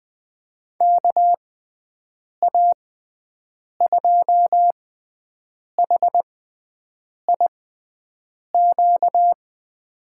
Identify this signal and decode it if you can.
Morse code, heard as a keyed tone, about 20 wpm, spaced as for 7 wpm, 700 Hz. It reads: KA2HIQ
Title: USA Call Signs - 6 Characters - 6x Character Spacing - 20wpm